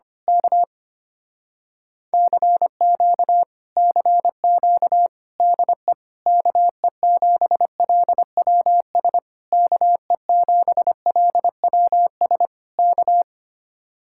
Morse code, heard as a keyed tone, about 24 wpm, 700 Hz. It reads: K CQ CQ DE KE7LWS KE7LWS K
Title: Speed-Racing Common QSO Phrases - 25wpm